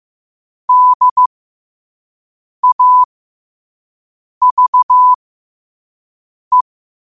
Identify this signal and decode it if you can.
Morse code, heard as a keyed tone, about 15 wpm, spaced as for 5 wpm, 1000 Hz. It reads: DAVE
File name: qotdfiles / 5wpm